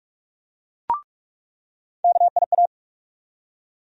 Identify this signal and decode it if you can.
Morse code, heard as a keyed tone, about 42 wpm, 700 Hz. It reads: KIA
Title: USA Call Sign Postfixes - 3 Letters 45wpm